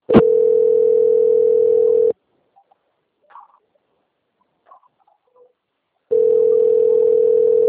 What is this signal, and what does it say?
Ringback tone (North America), 440 + 480 Hz.